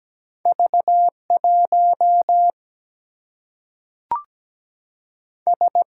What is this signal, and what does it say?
Morse code, heard as a keyed tone, about 17 wpm, 700 Hz. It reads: V1 S
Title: Single-Letter Single-Number 17wpm